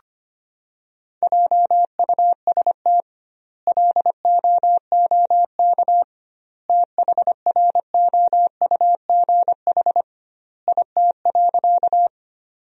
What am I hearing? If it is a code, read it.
Morse code, heard as a keyed tone, about 25 wpm, 700 Hz. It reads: JUST LOOK THROUGH IT.
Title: 2x Word-Spacing - Sentences from Top 200 Words - 25wpm